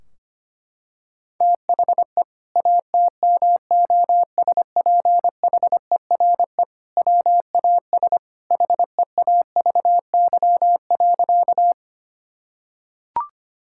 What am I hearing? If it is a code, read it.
Morse code, heard as a keyed tone, about 25 wpm, 700 Hz. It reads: THE ATMOSPHERE WAS HEAVY.
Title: Sentences from Top 1600 Words 25wpm